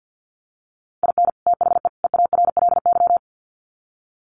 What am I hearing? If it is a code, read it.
Morse code, heard as a keyed tone, about 48 wpm, 700 Hz. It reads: IN THE EARLY
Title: Sets of 3 Words 50wpm